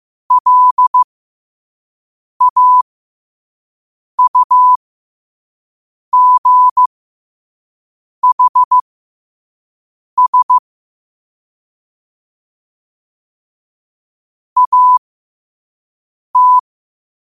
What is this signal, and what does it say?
Morse code, heard as a keyed tone, about 15 wpm, spaced as for 5 wpm, 1000 Hz. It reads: LAUGHS AT
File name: qotdfiles / 5wpm